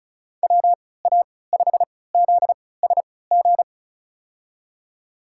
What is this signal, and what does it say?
Morse code, heard as a keyed tone, about 34 wpm, spaced as for 20 wpm, 700 Hz. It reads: WA5ZSG
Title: USA Call Signs - 3x Character Spacing - 35wpm